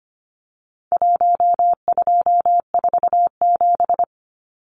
Morse code, heard as a keyed tone, about 25 wpm, 700 Hz. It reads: 1247
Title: Numbers - 4-Digit 25wpm